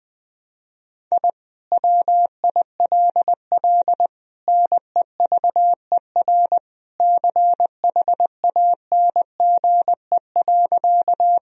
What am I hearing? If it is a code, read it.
Morse code, heard as a keyed tone, about 20 wpm, 700 Hz. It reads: I WILL NEVER CHANGE.
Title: Sentences from Top 1000 Words - Review Part 1 - 20wpm